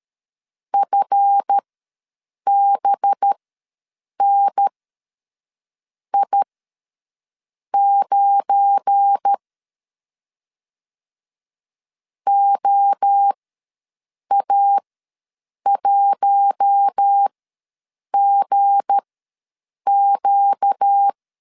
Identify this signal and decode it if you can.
Morse code, heard as a keyed tone, about 13 wpm, spaced as for 6 wpm, 800 Hz. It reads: FBNI9 OA1GQ
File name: LEZIONI / 7 GF9Z / 7 GF9 ws 12wpm 3344